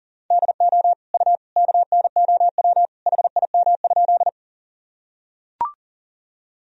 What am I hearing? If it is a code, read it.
Morse code, heard as a keyed tone, about 40 wpm, 700 Hz. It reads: DO U KNOW HIM?
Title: Sentences from Top 100 Words and Abbreviations 40wpm